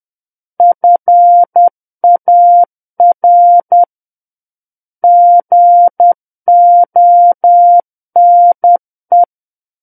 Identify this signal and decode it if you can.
Morse code, heard as a keyed tone, about 10 wpm, 700 Hz. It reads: FAR GONE
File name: Book One / 01-Chapter 1 - 10 WPM